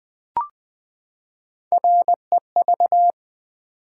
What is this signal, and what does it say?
Morse code, heard as a keyed tone, about 20 wpm, 700 Hz. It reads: REV
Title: Build A Word - 3 Letters At A Time - 8-Letter Words - 20wpm